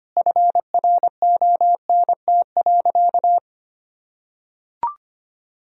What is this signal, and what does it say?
Morse code, heard as a keyed tone, about 25 wpm, 700 Hz. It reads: FRONT.
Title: Sentences from Top 1000 Words - Review Part 2 - 25wpm